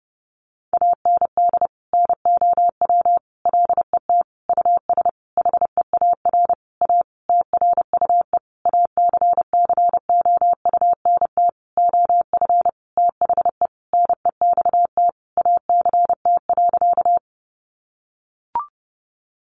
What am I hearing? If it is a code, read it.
Morse code, heard as a keyed tone, about 30 wpm, 700 Hz. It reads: AND NOW LET US HEAR A TRUE ACCOUNT OF THE NEXT ACT.
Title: Sentences from Top 900 Words 30wpm